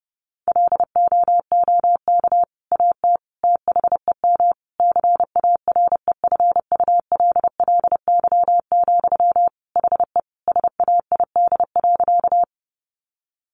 Morse code, heard as a keyed tone, about 30 wpm, 700 Hz. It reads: LOOK AT THEM CAREFULLY, HE SAID.
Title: Sentences from Top 1300 Words 30wpm